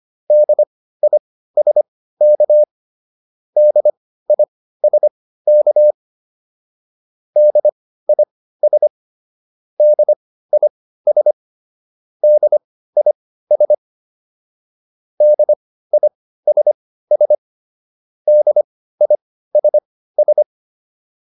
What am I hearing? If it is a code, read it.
Morse code, heard as a keyed tone, about 25 wpm, spaced as for 15 wpm, 600 Hz. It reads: DISK DISK DIS DIS DIS DISS DISS